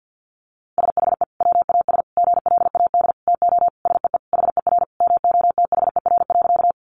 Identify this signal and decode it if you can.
Morse code, heard as a keyed tone, about 50 wpm, 700 Hz. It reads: SHE WAS GLAD TO SEE HER MOTHER.
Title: Sentences from Top 800 Words 50wpm